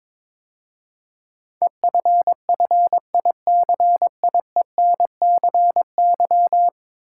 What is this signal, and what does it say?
Morse code, heard as a keyed tone, about 22 wpm, 700 Hz. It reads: EFFICIENCY
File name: practice / 22wpm / 10 Letter Words - First 3 Letters Given - 22wpm